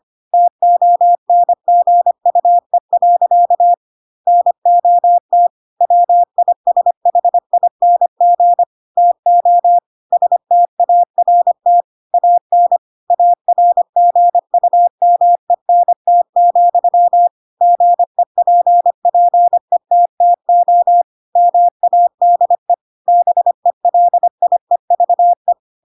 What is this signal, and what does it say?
Morse code, heard as a keyed tone, about 25 wpm, 700 Hz. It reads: TONGUE. NOT WISHING TO START AN ARGUMENT, GEPPETTO MADE BELIEVE
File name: Pinocchio / Pinocchio ch3 25 25 ik1hge